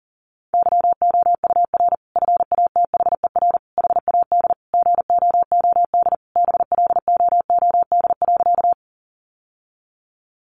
Morse code, heard as a keyed tone, about 40 wpm, 700 Hz. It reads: YOUR FATHER HAD GOOD BLOOD.
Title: Sentences from Top 1100 Words 40wpm